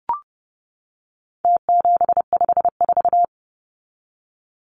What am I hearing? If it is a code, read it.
Morse code, heard as a keyed tone, about 30 wpm, 700 Hz. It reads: T754
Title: Numbers - Contest Style - 4-Digit 30wpm